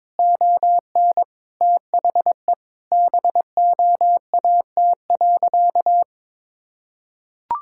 Morse code, heard as a keyed tone, about 22 wpm, 700 Hz. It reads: ON THE BOAT.
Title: Sentences from Top 1000 Words - Review Part 5 - 22wpm